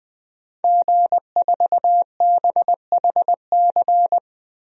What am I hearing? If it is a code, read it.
Morse code, heard as a keyed tone, about 20 wpm, 700 Hz. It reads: G4BHC
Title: DXCC-100 Call Signs - 5 Characters - 20wpm